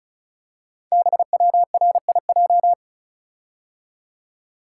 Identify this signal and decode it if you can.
Morse code, heard as a keyed tone, about 35 wpm, 690 Hz. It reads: BWRIJ
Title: Code Groups - 5 Letters - 35wpm